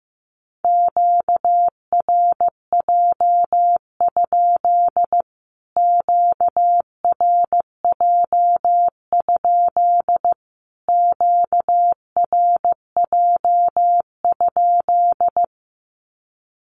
Morse code, heard as a keyed tone, about 15 wpm, 700 Hz. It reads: QRJ? QRJ? QRJ?
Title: Q-Codes Repeated 3x 15wpm